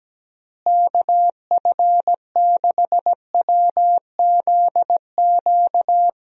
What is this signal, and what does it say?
Morse code, heard as a keyed tone, about 17 wpm, 700 Hz. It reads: KF6WZQ
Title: USA Call Signs - 6 Characters - Repeated 3x - 17wpm